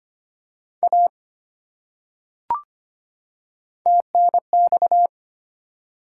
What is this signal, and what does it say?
Morse code, heard as a keyed tone, about 25 wpm, 700 Hz. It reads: A TNX
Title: Top 180 QSO Elements 25wpm